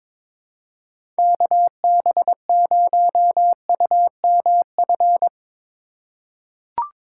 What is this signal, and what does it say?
Morse code, heard as a keyed tone, about 22 wpm, 700 Hz. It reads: KB0UMF
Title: USA Call Signs - 6 Characters - 4x Character Spacing - 22wpm